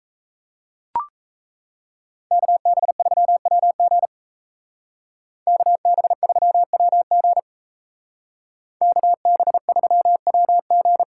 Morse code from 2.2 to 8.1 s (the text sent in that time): KB3WG KB3WG